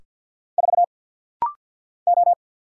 Morse code, heard as a keyed tone, about 50 wpm, 700 Hz. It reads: 4 O
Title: Single Letter-Number - Rapid-Fire 50wpm